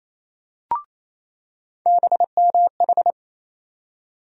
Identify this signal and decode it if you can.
Morse code, heard as a keyed tone, about 28 wpm, 700 Hz. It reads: BMH
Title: USA Call Sign Postfixes - 3 Letters 28wpm